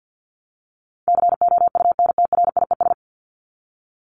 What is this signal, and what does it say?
Morse code, heard as a keyed tone, about 50 wpm, 700 Hz. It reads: COUNTRIES
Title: ES:S Suffix - Top 100 Words with Base Words 50wpm